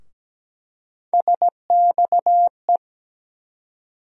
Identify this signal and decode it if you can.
Morse code, heard as a keyed tone, about 17 wpm, 700 Hz. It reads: SXE
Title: USA Call Sign Postfixes - 3 Letters 17wpm